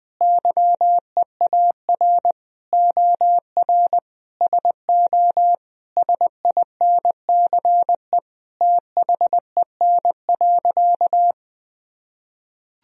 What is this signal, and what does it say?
Morse code, heard as a keyed tone, about 19 wpm, 700 Hz. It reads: YEAR OR SO SINCE THEN.
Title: Sentences from Top 1000 Words - Review Part 1 - 20wpm